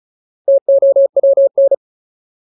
Morse code, heard as a keyed tone, about 34 wpm, 550 Hz. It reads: TOWN
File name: Ultimate Word List / 35WPM Ultimate